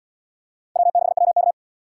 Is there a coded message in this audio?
Yes: Morse code from 0.8 to 1.5 s, heard as a keyed tone, about 68 wpm, 700 Hz. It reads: W6WK